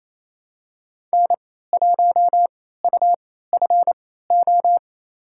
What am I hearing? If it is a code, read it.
Morse code, heard as a keyed tone, about 28 wpm, spaced as for 16 wpm, 700 Hz. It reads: N1UFO
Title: USA Call Signs - 3x Character Spacing - 28wpm